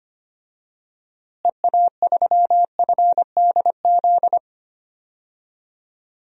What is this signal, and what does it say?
Morse code, heard as a keyed tone, about 25 wpm, 700 Hz. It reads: EA3FDZ
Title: DXCC-100 Call Signs - 6 Characters - Repeated 3x - 25wpm